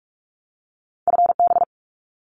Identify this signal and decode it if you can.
Morse code, heard as a keyed tone, about 44 wpm, 700 Hz. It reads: FB
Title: Abbreviations 45wpm